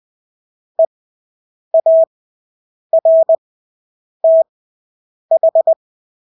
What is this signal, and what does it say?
Morse code, heard as a keyed tone, about 20 wpm, spaced as for 8 wpm, 650 Hz. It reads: EARTH
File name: Engelske ord 20/8 wpm